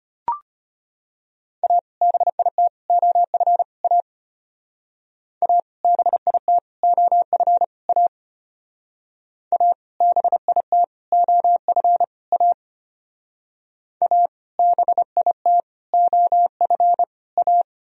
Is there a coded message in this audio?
Yes: Morse code from 1.6 to 17.6 s, heard as a keyed tone, about 33 wpm, 700 Hz. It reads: A BIT OF A A BIT OF A A BIT OF A A BIT OF A